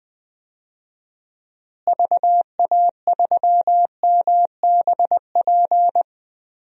Morse code, heard as a keyed tone, about 20 wpm, 700 Hz. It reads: VA3MBP
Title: DXCC-100 Call Signs - 6 Characters - Repeated 2x - 20wpm